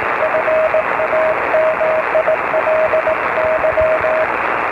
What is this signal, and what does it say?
Morse code, heard as a keyed tone, about 18 wpm, 620 Hz, with radio noise. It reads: FAMILY